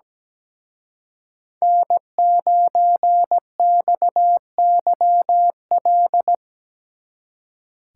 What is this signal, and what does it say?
Morse code, heard as a keyed tone, about 20 wpm, 700 Hz. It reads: N9XYL
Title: Speed Racing - USA 1x3 Call Signs 17wpm